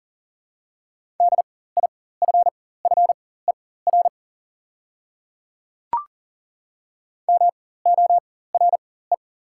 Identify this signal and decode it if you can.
Morse code, heard as a keyed tone, about 37 wpm, spaced as for 18 wpm, 700 Hz. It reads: DIFFER MORE
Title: 4x Character Spacing - Top 200 Words - 40wpm